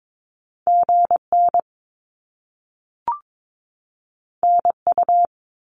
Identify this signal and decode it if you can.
Morse code, heard as a keyed tone, about 22 wpm, 700 Hz. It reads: GN NU
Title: Top Two Letter Combinations in Words 22wpm